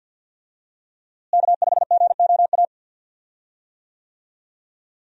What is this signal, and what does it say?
Morse code, heard as a keyed tone, about 52 wpm, 700 Hz. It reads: K5GOA
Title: USA Call Signs - 5 Characters - Repeated 3x - 50wpm